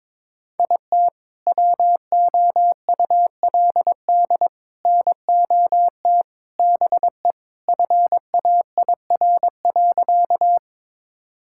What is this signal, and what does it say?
Morse code, heard as a keyed tone, about 21 wpm, 700 Hz. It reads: IT WOULD NOT BE FAIR.
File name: practice / 22wpm / Sentences with 3-Word Phrases - 22wpm